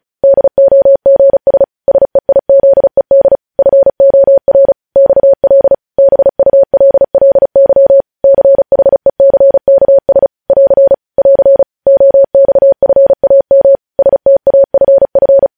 Morse code, heard as a keyed tone, about 36 wpm, 550 Hz. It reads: DOGS SEIZED FOR XL BULLY CHECKS <AR> <AR> OXFAM STAFF
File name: podcast / News headlines-1x-35x35